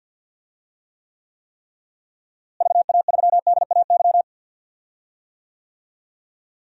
Morse code, heard as a keyed tone, about 48 wpm, 700 Hz. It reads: VA3DAY
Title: DXCC-100 Call Signs - 6 Characters - Repeated 3x - 50wpm